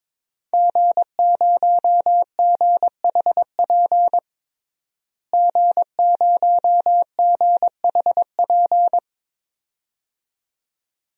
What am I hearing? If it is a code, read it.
Morse code, heard as a keyed tone, about 21 wpm, 700 Hz. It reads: G0GHP G0GHP
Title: DXCC-100 Call Signs - 5 Characters - Repeated 2x - 22wpm